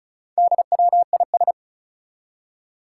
Morse code, heard as a keyed tone, about 36 wpm, 700 Hz. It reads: DWIS
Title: Four Challenging Letter-Number Combinations 35wpm